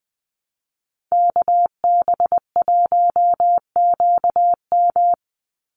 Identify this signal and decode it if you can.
Morse code, heard as a keyed tone, about 20 wpm, 700 Hz. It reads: KB1QM
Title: USA Call Signs 20wpm